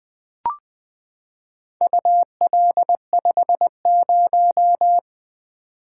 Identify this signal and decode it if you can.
Morse code, heard as a keyed tone, about 20 wpm, 700 Hz. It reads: UL50
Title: Four Letter-Number Combinations - 20wpm